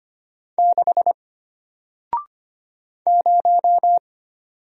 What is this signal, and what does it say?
Morse code, heard as a keyed tone, about 25 wpm, 700 Hz. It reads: 6 0
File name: Single-Letter-Number-25wpm